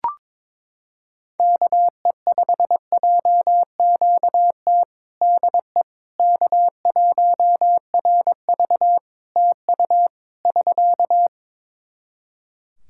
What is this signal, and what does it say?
Morse code, heard as a keyed tone, about 21 wpm, 700 Hz. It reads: KE5JQT DE K1RV TU <SK>
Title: Practice QSO - Exchange 4 - Ending - 22wpm